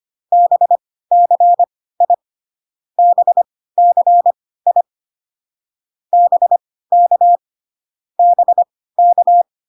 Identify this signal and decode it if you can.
Morse code, heard as a keyed tone, about 25 wpm, spaced as for 16 wpm, 700 Hz. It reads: BCI BCI BK BK